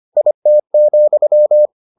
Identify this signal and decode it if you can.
Morse code, heard as a keyed tone, about 25 wpm, 600 Hz. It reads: IT,